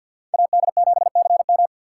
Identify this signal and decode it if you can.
Morse code, heard as a keyed tone, about 62 wpm, 700 Hz. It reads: AD7CM
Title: Speed Racing - USA 2x2 Call Signs 50wpm